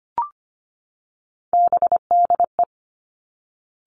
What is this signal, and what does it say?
Morse code, heard as a keyed tone, about 25 wpm, 700 Hz. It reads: BDE
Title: Three Challenging Letter-Number Combinations 25wpm